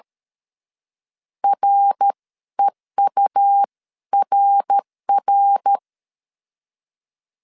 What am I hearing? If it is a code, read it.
Morse code, heard as a keyed tone, about 13 wpm, spaced as for 11 wpm, 800 Hz. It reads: REURR
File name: CW / DET / 1 3 1500C S B 12wpm 2828